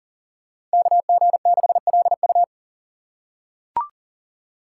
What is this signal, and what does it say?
Morse code, heard as a keyed tone, about 40 wpm, 700 Hz. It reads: KG6LU
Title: USA Call Signs - 5 Characters - 3x Character Spacing - 40wpm